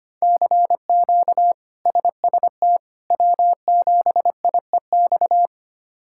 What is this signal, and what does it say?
Morse code, heard as a keyed tone, about 25 wpm, 700 Hz. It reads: CQ SST W7IEX
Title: K1USN SST QSO Practice - 25wpm